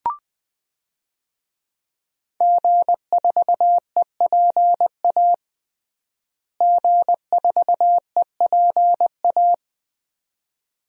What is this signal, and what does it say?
Morse code, heard as a keyed tone, about 21 wpm, 700 Hz. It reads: G4EPA G4EPA
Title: DXCC-100 Call Signs - 5 Characters - Repeated 2x - 20wpm